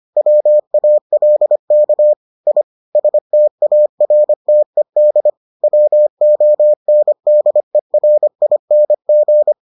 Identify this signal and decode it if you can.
Morse code, heard as a keyed tone, about 25 wpm, 600 Hz. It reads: WALK I STARTED WONDERING